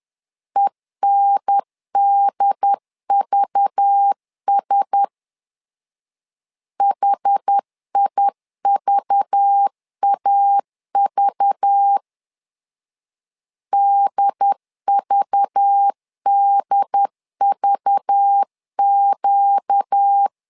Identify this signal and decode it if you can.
Morse code, heard as a keyed tone, about 10 wpm, 800 Hz. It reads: ENDVS HIVAV DVDVQ
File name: LEZIONI / 5 LBVC / 5 LBV 10wpm 2248